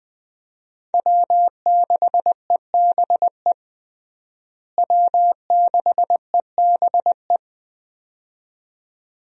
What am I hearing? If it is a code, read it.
Morse code, heard as a keyed tone, about 20 wpm, 700 Hz. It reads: W6EBE W6EBE
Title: USA Call Signs - 5 Characters - Repeated 2x - 20wpm